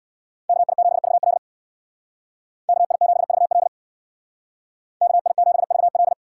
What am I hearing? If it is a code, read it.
Morse code, heard as a keyed tone, about 64 wpm, 700 Hz. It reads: KI7VL KI7VL KI7VL